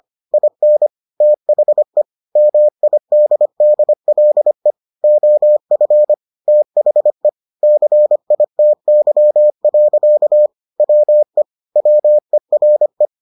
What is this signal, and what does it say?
Morse code, heard as a keyed tone, about 25 wpm, 600 Hz. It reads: IN THE MIDDLE OF THE CITY. WE WERE